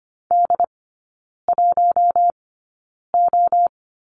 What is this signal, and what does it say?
Morse code, heard as a keyed tone, about 25 wpm, spaced as for 9 wpm, 700 Hz. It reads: D1O
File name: course / 25:10wpm / 093 Lesson 16e - Letter c - Three Characters 25wpm